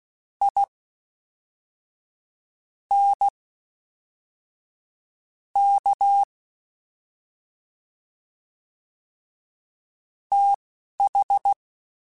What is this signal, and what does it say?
Morse code, heard as a keyed tone, about 16 wpm, spaced as for 5 wpm, 800 Hz. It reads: INK TH